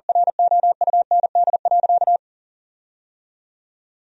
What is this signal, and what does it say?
Morse code, heard as a keyed tone, about 40 wpm, 700 Hz. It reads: ROUND.
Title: 4x Word-Spacing - Sentences from Top 500 Words - 40wpm